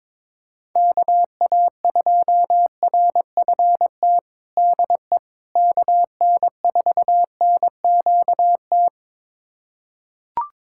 Morse code, heard as a keyed tone, about 22 wpm, 700 Hz. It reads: KA2RFT DE KN4NQT
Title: USA Callsign DE Callsign 22wpm